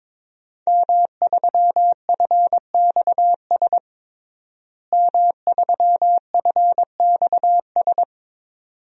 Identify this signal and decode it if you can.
Morse code, heard as a keyed tone, about 21 wpm, 700 Hz. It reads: M3FXS M3FXS
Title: DXCC-100 Call Signs - 5 Characters - Repeated 2x - 22wpm